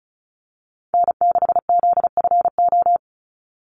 Morse code, heard as a keyed tone, about 34 wpm, 700 Hz. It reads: N6ZFO
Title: Call Signs - 35wpm